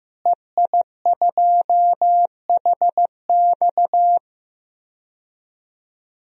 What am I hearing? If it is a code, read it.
Morse code, heard as a keyed tone, about 15 wpm, 700 Hz. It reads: EI2HX
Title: DXCC-100 Call Signs - 5 Characters - Repeated 2x - 15wpm